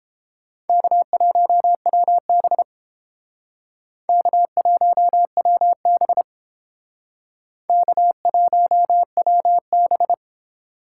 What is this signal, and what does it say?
Morse code, heard as a keyed tone, about 31 wpm, 700 Hz. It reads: K1WB K1WB K1WB